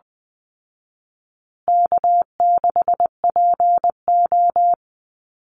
Morse code, heard as a keyed tone, about 20 wpm, 700 Hz. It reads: K6PO
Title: Call Signs Repeated 2x - 20wpm